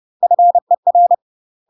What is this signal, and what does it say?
Morse code, heard as a keyed tone, about 30 wpm, 700 Hz. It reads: FER
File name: UK MASTER QSOs TRAINING / 30WPM / UK MASTER QSOs 0048